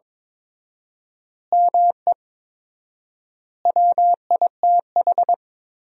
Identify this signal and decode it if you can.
Morse code, heard as a keyed tone, about 22 wpm, 700 Hz. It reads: ME WITH
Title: 4x Word-Spacing - Sentences from Top 500 Words - 22wpm